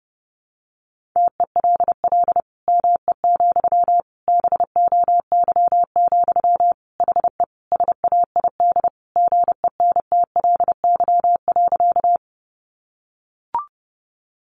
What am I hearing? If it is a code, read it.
Morse code, heard as a keyed tone, about 30 wpm, 700 Hz. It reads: TELL ME, BOY, HE SAID GENTLY.
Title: Sentences from Top 1500 Words 30wpm